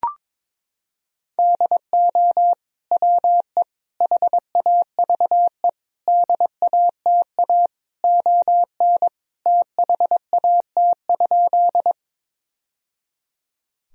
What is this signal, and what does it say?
Morse code, heard as a keyed tone, about 22 wpm, 690 Hz. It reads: DO WE HAVE DATA ON THAT?
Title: Sentences from Top 1000 Words - Review Part 3 - 22wpm